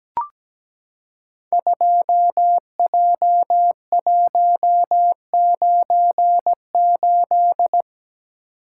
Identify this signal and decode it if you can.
Morse code, heard as a keyed tone, about 17 wpm, 700 Hz. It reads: 2J198
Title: Code Groups - 5 Letter:Numbers - 17wpm